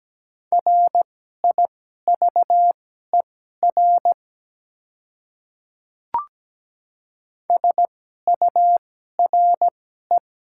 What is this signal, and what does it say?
Morse code, heard as a keyed tone, about 17 wpm, spaced as for 12 wpm, 700 Hz. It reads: RIVER SURE
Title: 2x Character Spacing - Top 300 Words - 17wpm